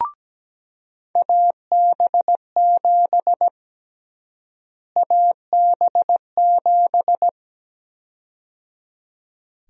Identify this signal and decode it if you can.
Morse code, heard as a keyed tone, about 17 wpm, 700 Hz. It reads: AB7 AB7
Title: USA Call Sign Prefixes - Repeated 2x - 17wpm